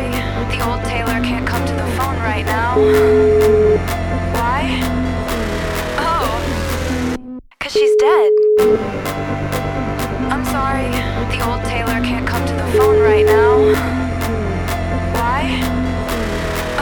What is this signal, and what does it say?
Ringback tone (Europe, 425 Hz), with speech.